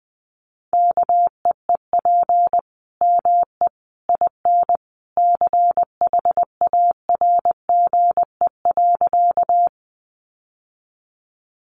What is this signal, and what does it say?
Morse code, heard as a keyed tone, about 20 wpm, 700 Hz. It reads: KEEP ME IN CHARGE.
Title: Sentences from Top 900 Words 20wpm